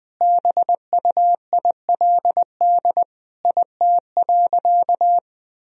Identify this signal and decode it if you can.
Morse code, heard as a keyed tone, about 20 wpm, 700 Hz. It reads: BUILD IT.
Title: Sentences from Top 1000 Words - Review Part 1 - 20wpm